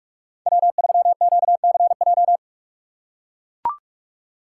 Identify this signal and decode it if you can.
Morse code, heard as a keyed tone, about 44 wpm, 700 Hz. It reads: W3QCJ
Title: USA Call Signs - 5 Characters - 6x Character Spacing - 45wpm